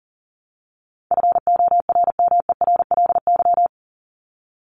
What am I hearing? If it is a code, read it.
Morse code, heard as a keyed tone, about 40 wpm, 700 Hz. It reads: FORMERLY
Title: LY Suffix - Top 100 Words with Base Words 40wpm